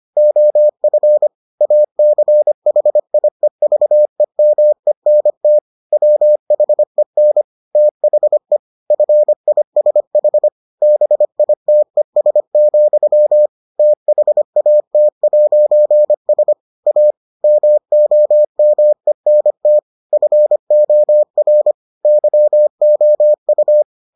Morse code, heard as a keyed tone, about 25 wpm, 600 Hz. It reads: OF ACHIEVEMENT WHEN THE FISH BITES, THAT'S A MOMENT FOR YOU